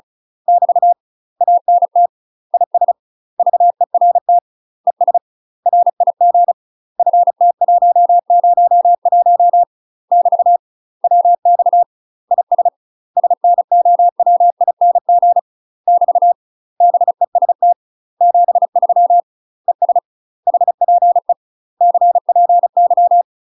Morse code, heard as a keyed tone, about 36 wpm, 700 Hz. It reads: <BT> ANT IS VERT ES RIG FT101 <BT> WX IS SNOWING <BT> BEST 73 ES HPE CPY
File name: UK MASTER QSOs TRAINING / 35WPM / UK MASTER QSOs 0001